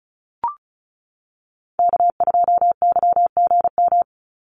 Morse code, heard as a keyed tone, about 34 wpm, 700 Hz. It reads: K2YGM
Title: Call Signs - 35wpm